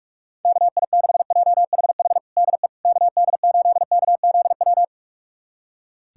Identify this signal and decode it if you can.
Morse code, heard as a keyed tone, about 46 wpm, 700 Hz. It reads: KI6JHH DE KD8KZW
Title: USA Callsign DE Callsign 45wpm